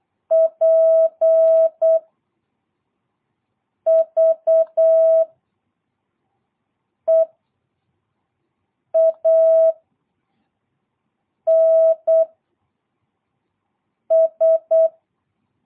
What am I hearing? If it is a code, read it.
Morse code, heard as a keyed tone, about 8 wpm, 650 Hz. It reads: P V E A N S
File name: jh8cbh / 2022cw